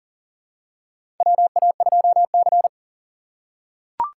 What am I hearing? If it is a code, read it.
Morse code, heard as a keyed tone, about 40 wpm, 700 Hz. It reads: WA2C
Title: USA Call Signs - 4x Character Spacing - 40wpm